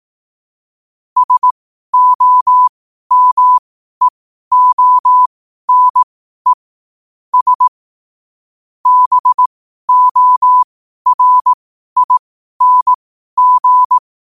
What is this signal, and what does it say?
Morse code, heard as a keyed tone, about 18 wpm, spaced as for 12 wpm, 1000 Hz. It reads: SOMEONE S BORING
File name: qotdfiles / 12wpm